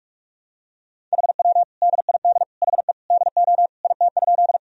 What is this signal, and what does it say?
Morse code, heard as a keyed tone, about 43 wpm, 700 Hz. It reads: HW DID HE DO IT?
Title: Sentences from Top 100 Words and Abbreviations 45wpm